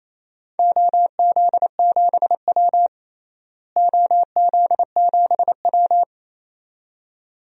Morse code, heard as a keyed tone, about 28 wpm, 700 Hz. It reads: OZ7W OZ7W
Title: DXCC-100 Call Signs - 4 Characters - Repeated 2x - 28wpm